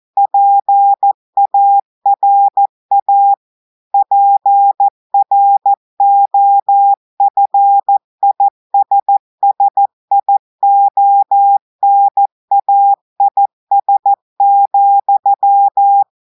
Morse code, heard as a keyed tone, about 14 wpm, 800 Hz. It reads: PARA PROFISSIONAIS,
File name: audios / portugais / portugais texte en clair 1 vitesse 14 mots minutes